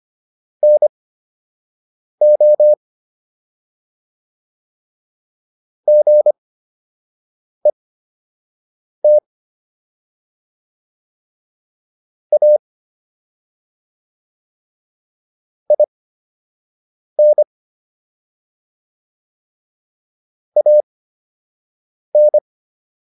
Morse code, heard as a keyed tone, about 25 wpm, spaced as for 6 wpm, 600 Hz. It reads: NO GET A IN AN